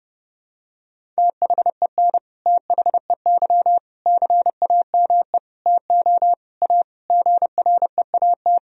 Morse code, heard as a keyed tone, about 32 wpm, 700 Hz. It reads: THEN THEY CAME TO A GREAT
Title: Sentences from Top 1000 Words - Review Part 1 - 30wpm